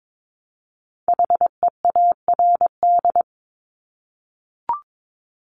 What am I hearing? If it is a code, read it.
Morse code, heard as a keyed tone, about 22 wpm, 700 Hz. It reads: HEARD
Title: Multi-letter Phonemes Repeated 3x 22wpm